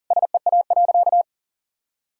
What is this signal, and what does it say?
Morse code, heard as a keyed tone, about 40 wpm, 700 Hz. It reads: SEA.
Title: Sentences from Top 1000 Words - Review Part 5 - 40wpm